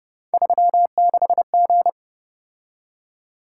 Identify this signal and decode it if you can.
Morse code, heard as a keyed tone, about 29 wpm, 700 Hz. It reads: 36G
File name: Speed-Racing - Three Letter-Number Combinations - 25wpm